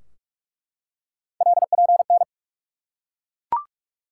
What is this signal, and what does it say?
Morse code, heard as a keyed tone, about 45 wpm, 700 Hz. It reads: LPN
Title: Practice QSO - Exchange 3 - More Info - 45wpm